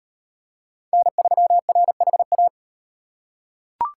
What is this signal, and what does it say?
Morse code, heard as a keyed tone, about 38 wpm, 700 Hz. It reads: N3RHA